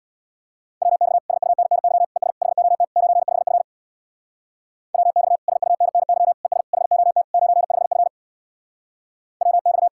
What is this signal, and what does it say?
Morse code, heard as a keyed tone, about 69 wpm, 700 Hz. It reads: WX SUNNY ES HOT 95F WX SUNNY ES HOT 95F WX